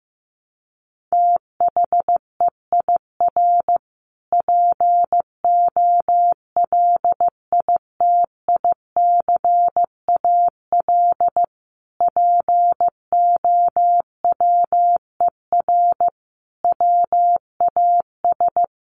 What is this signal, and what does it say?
Morse code, heard as a keyed tone, about 14 wpm, 700 Hz. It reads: THEIR POLITICAL POWER WAS
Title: Sentences from Top 1300 Words 15wpm